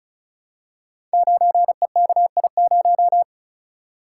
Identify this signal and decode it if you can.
Morse code, heard as a keyed tone, about 36 wpm, 700 Hz. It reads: 9EKI0
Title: Code Groups - 5 Letter:Numbers - 35wpm